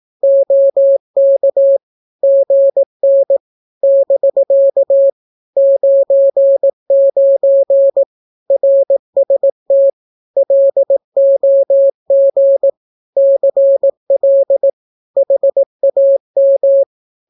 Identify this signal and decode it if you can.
Morse code, heard as a keyed tone, about 18 wpm, 550 Hz. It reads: OK GN <BK> 99 RST LOG CL HAM
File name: naukapodstaw / skroty / skroty